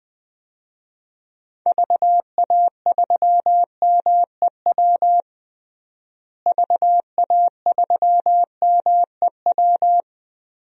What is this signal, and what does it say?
Morse code, heard as a keyed tone, about 20 wpm, 700 Hz. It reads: VA3MEW VA3MEW
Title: DXCC-100 Call Signs - 6 Characters - Repeated 3x - 20wpm